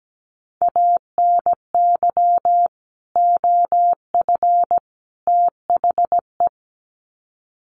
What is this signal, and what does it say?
Morse code, heard as a keyed tone, about 17 wpm, 700 Hz. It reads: ANY OF THE
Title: Sets of 3 Words 17wpm